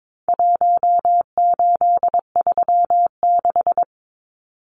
Morse code, heard as a keyed tone, about 22 wpm, 700 Hz. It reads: 1836
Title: Numbers - Contest Style - 4-Digit 22wpm